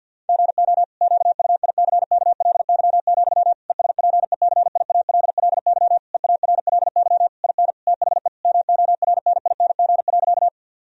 Morse code, heard as a keyed tone, about 50 wpm, 700 Hz. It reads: GO QUICKLY, ESPECIALLY EARLY IN THE MORNING.
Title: Sentences from Top 1000 Words - Review Part 4 - 50wpm